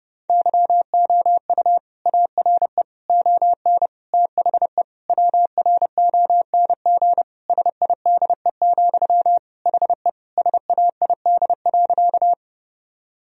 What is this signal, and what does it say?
Morse code, heard as a keyed tone, about 30 wpm, 700 Hz. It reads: YOU ARE ON THE WRONG SIDE, HE SAID.
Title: Sentences from Top 1000 Words - Review Part 1 - 30wpm